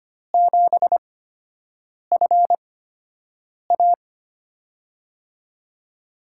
Morse code, heard as a keyed tone, about 25 wpm, spaced as for 7 wpm, 700 Hz. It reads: 7FA
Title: USA Call Signs - 5 Characters - 8x Character Spacing - 25wpm